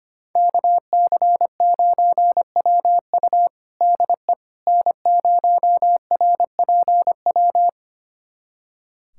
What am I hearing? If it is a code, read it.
Morse code, heard as a keyed tone, about 25 wpm, 700 Hz. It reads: KC9WU DE N0RPW
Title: USA Callsign DE Callsign - 5 Characters - 25wpm